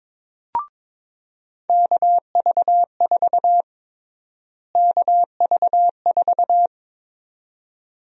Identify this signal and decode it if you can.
Morse code, heard as a keyed tone, about 22 wpm, 700 Hz. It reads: KV4 KV4
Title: USA Call Sign Prefixes - Repeated 2x - 22wpm